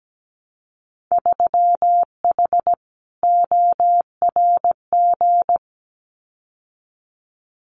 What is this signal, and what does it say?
Morse code, heard as a keyed tone, about 17 wpm, 700 Hz. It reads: 3H ORG
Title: Winter Field Day Exchanges Encore 17wpm